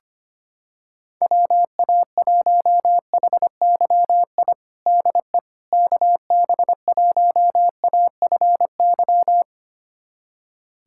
Morse code, heard as a keyed tone, about 25 wpm, 700 Hz. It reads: WA1HYI DE KB1AFY
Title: USA Callsign DE Callsign - 6 Characters - 25wpm